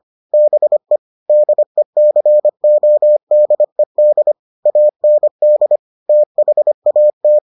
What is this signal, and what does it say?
Morse code, heard as a keyed tone, about 25 wpm, 600 Hz. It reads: BE DECODED AND THAT